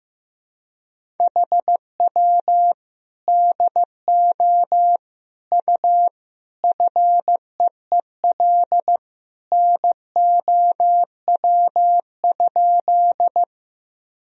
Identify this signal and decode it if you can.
Morse code, heard as a keyed tone, about 16 wpm, 700 Hz. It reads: HW DO U FEEL NOW?
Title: Sentences from Top 400 Words and Abbreviations 15wpm